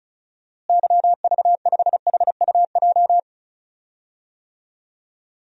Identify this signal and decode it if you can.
Morse code, heard as a keyed tone, about 36 wpm, 700 Hz. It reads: YV5HUJ
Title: DXCC-100 Call Signs - 6 Characters - Repeated 2x - 35wpm